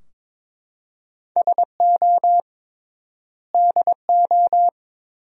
Morse code, heard as a keyed tone, about 22 wpm, 700 Hz. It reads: SO DO
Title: 3x Word-Spacing - Sentences from Top 400 Words - 22wpm